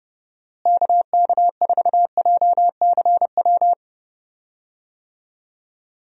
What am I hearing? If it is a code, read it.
Morse code, heard as a keyed tone, about 30 wpm, 700 Hz. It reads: KK4JCW
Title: USA Call Signs - 6 Characters - Repeated 2x - 30wpm